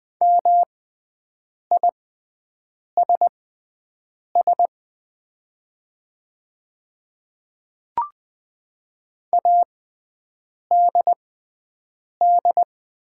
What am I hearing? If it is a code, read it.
Morse code, heard as a keyed tone, about 20 wpm, spaced as for 7 wpm, 700 Hz. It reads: MISS ADD
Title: 6x Character Spacing - Top 500 Words - 20wpm